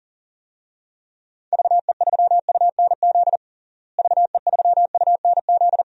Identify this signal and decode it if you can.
Morse code, heard as a keyed tone, about 40 wpm, 700 Hz. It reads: VE3UNZ VE3UNZ
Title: DXCC-100 Call Signs - 6 Characters - Repeated 2x - 40wpm